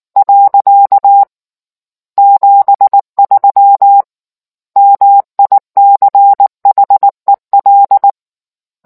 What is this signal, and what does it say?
Morse code, heard as a keyed tone, about 19 wpm, 800 Hz. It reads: . 73 MICHEL